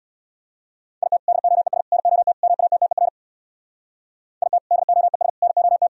Morse code, heard as a keyed tone, about 64 wpm, 700 Hz. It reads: IT DOES NOT MATTER IT DOES NOT